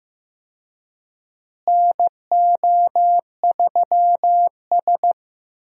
Morse code, heard as a keyed tone, about 15 wpm, 700 Hz. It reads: NO3S
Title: USA Call Signs - 4 Characters - Repeated 3x - 15wpm